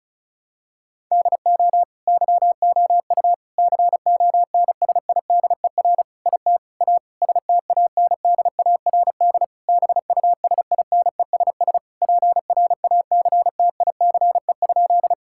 Morse code, heard as a keyed tone, about 36 wpm, 700 Hz. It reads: DO YOU CONSIDER IT A STANDARD BUSINESS PRACTICE?
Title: Sentences from Top 1000 Words - Review Part 6 - 35wpm